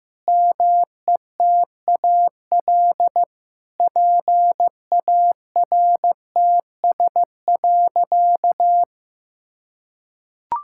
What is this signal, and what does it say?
Morse code, heard as a keyed tone, about 15 wpm, 700 Hz. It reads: METAL PARTS.
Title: Sentences from Top 1000 Words - Review Part 6 - 15wpm